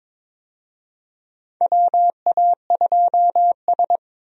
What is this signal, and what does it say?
Morse code, heard as a keyed tone, about 23 wpm, 700 Hz. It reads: WA2S
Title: USA Call Signs - 4 Characters - 22wpm